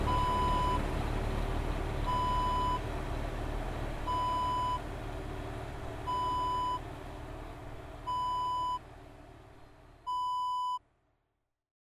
There is no code.